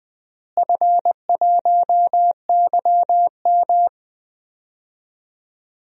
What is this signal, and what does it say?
Morse code, heard as a keyed tone, about 20 wpm, 700 Hz. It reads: F1YM
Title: DXCC-100 Call Signs - 4 Characters - Repeated 3x - 20wpm